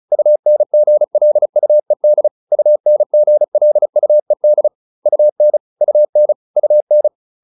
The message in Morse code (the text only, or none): UNGLUED UNGLUED UN UN UN